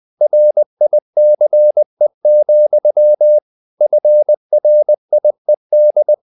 Morse code, heard as a keyed tone, about 20 wpm, 600 Hz. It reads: RICE, FRIED
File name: the old man and the sea / the old man and the sea part 1 20wpm